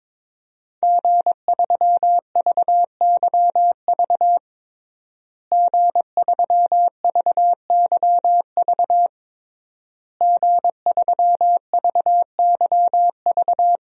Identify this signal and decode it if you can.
Morse code, heard as a keyed tone, about 22 wpm, 700 Hz. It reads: G3VYV G3VYV G3VYV
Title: DXCC-100 Call Signs - 5 Characters - Repeated 3x - 22wpm